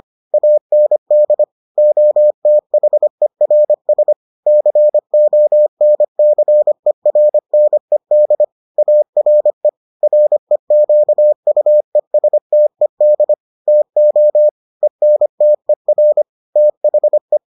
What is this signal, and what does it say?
Morse code, heard as a keyed tone, about 25 wpm, 600 Hz. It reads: AND OTHERS CONCERNED ARE REQUESTED TO ENTER THE